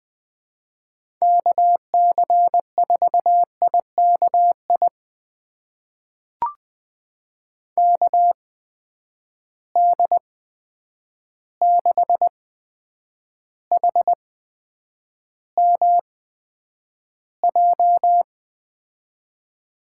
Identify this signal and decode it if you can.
Morse code, heard as a keyed tone, about 20 wpm, spaced as for 9 wpm, 700 Hz. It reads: KC4IKI KD6HMJ
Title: USA Call Signs - 6 Characters - 8x Character Spacing - 20wpm